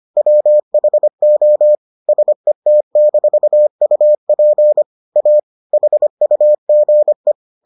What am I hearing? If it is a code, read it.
Morse code, heard as a keyed tone, about 25 wpm, 600 Hz. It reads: WHO SET-UP A HUGE